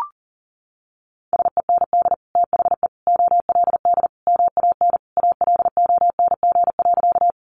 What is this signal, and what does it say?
Morse code, heard as a keyed tone, about 40 wpm, 700 Hz. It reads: SEND THE OLD MAN ALONG.
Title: Sentences from Top 1500 Words 40wpm